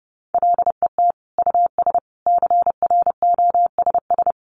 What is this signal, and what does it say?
Morse code, heard as a keyed tone, about 30 wpm, 700 Hz. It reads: LET US CROSS
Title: Sentences from Top 300 Words 30wpm